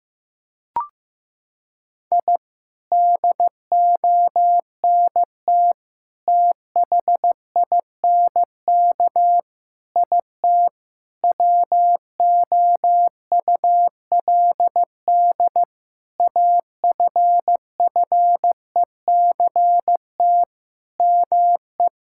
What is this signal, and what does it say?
Morse code, heard as a keyed tone, about 15 wpm, 700 Hz. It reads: I DONT THINK IT WOULD AFFECT ME
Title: Sentences from Top 1000 Words - Review Part 6 - 15wpm